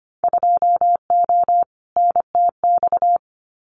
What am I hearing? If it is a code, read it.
Morse code, heard as a keyed tone, about 25 wpm, 700 Hz. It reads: 2O NTX
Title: Winter Field Day Exchanges Encore 25wpm